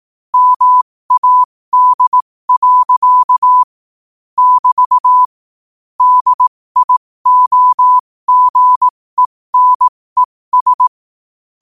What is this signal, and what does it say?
Morse code, heard as a keyed tone, about 18 wpm, spaced as for 15 wpm, 1000 Hz. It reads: MAD. <BT> DIOGENES
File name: qotdfiles / 15wpm